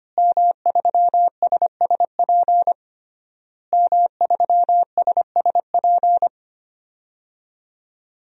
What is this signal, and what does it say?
Morse code, heard as a keyed tone, about 25 wpm, 700 Hz. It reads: M3SSP M3SSP
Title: DXCC-100 Call Signs - 5 Characters - Repeated 3x - 25wpm